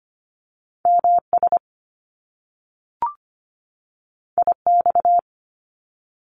Morse code, heard as a keyed tone, about 25 wpm, 700 Hz. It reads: MS IX
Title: Top Two Letter Combinations in Words 25wpm